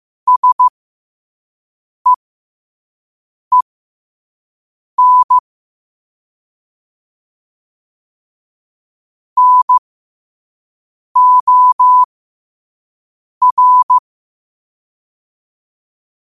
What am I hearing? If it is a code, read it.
Morse code, heard as a keyed tone, about 15 wpm, spaced as for 5 wpm, 1000 Hz. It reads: SEEN NOR
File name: qotdfiles / 5wpm